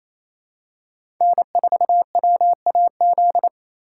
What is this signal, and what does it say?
Morse code, heard as a keyed tone, about 28 wpm, 700 Hz. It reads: N4WAZ
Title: USA Call Signs - 8x Character Spacing - 28wpm